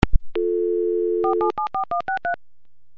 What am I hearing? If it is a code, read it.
DTMF, heard as telephone touch tones, tones of about 120 ms. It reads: 4774163
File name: Modem Sounds / unk unk v92